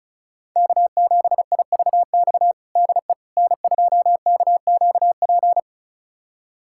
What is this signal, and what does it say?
Morse code, heard as a keyed tone, about 34 wpm, 700 Hz. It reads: K7IVX DE N2KQP
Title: USA Callsign DE Callsign - 5 Characters - 35wpm